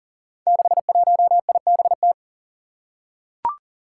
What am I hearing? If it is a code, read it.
Morse code, heard as a keyed tone, about 40 wpm, 700 Hz. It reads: 61IBT
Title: Five Challenging Letter-Number Combinations 40wpm